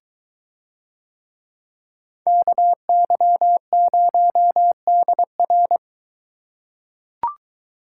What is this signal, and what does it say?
Morse code, heard as a keyed tone, about 20 wpm, 700 Hz. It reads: KY0DR